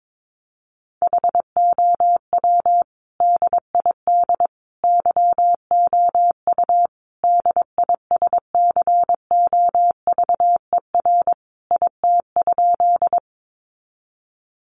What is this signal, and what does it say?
Morse code, heard as a keyed tone, about 22 wpm, 700 Hz. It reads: HOW DID YOU DISCOVER IT?
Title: Sentences from Top 1100 Words 22wpm